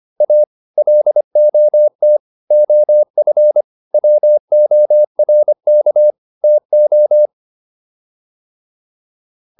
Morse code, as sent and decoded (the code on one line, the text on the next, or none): .- / .-.. --- - / --- ..-. / .-- --- .-. -.- / - ---
A LOT OF WORK TO